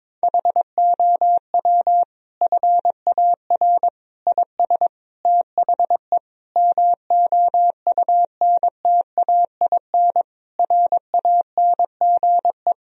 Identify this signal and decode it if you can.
Morse code, heard as a keyed tone, about 22 wpm, 700 Hz. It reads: HOW FAR IS THE MOUNTAIN RANGE
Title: Sentences from Top 1000 Words - Review Part 4 - 22wpm